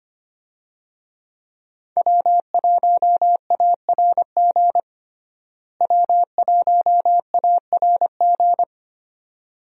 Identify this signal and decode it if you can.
Morse code, heard as a keyed tone, about 25 wpm, 700 Hz. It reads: W1ARG W1ARG
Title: USA Call Signs - 5 Characters - Repeated 2x - 25wpm